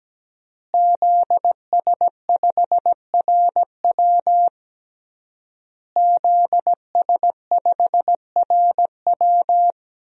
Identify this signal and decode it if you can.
Morse code, heard as a keyed tone, about 17 wpm, 700 Hz. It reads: ZS5RW ZS5RW
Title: DXCC-100 Call Signs - 5 Characters - Repeated 3x - 17wpm